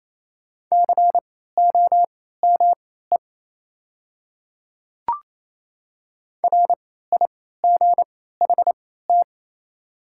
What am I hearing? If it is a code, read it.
Morse code, heard as a keyed tone, about 28 wpm, spaced as for 16 wpm, 700 Hz. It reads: COME RIGHT
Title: 3x Character Spacing - Top 200 Words - 28wpm